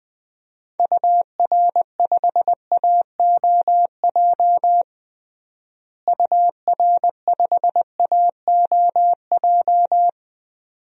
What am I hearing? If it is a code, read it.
Morse code, heard as a keyed tone, about 20 wpm, 700 Hz. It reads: UR5AOJ UR5AOJ
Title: DXCC-100 Call Signs - 6 Characters - Repeated 2x - 20wpm